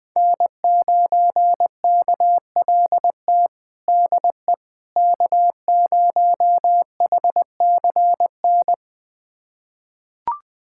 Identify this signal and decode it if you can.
Morse code, heard as a keyed tone, about 20 wpm, 700 Hz. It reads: N9KLT DE K0HCN
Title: USA Callsign DE Callsign - 5 Characters - 20wpm